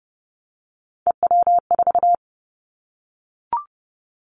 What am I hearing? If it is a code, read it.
Morse code, heard as a keyed tone, about 30 wpm, 700 Hz. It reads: EW4
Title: Callsign Prefixes for Top 50 DXCC - 30wpm